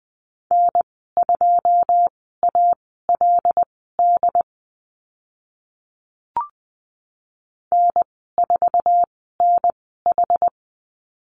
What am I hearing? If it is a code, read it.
Morse code, heard as a keyed tone, about 20 wpm, spaced as for 14 wpm, 700 Hz. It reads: N2ALD N4NH
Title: 082 Lesson 14g - Number 2 - Call Signs 20wpm